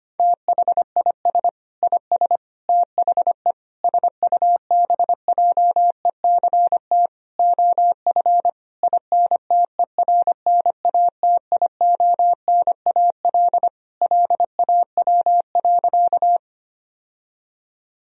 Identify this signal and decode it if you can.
Morse code, heard as a keyed tone, about 25 wpm, 700 Hz. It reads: THIS IS THE SUBJECT OF INTERNATIONAL LAW.
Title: Sentences from Top 1000 Words - Review Part 4 - 25wpm